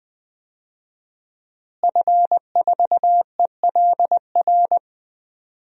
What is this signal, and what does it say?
Morse code, heard as a keyed tone, about 20 wpm, 700 Hz. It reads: F4ELR
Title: DXCC-100 Call Signs - 5 Characters - Repeated 3x - 20wpm